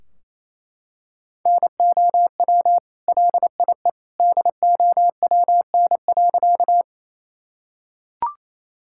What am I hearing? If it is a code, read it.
Morse code, heard as a keyed tone, about 28 wpm, 700 Hz. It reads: NOW LIE DOWN.
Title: Sentences from Top 1600 Words 28wpm